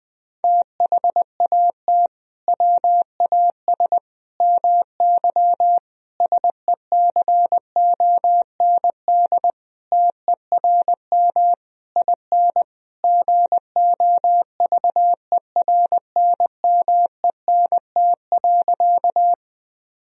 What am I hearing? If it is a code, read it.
Morse code, heard as a keyed tone, about 20 wpm, 700 Hz. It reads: THAT WAS MY SECOND TERM IN GOVERNMENT.
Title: Sentences from Top 1000 Words - Review Part 4 - 20wpm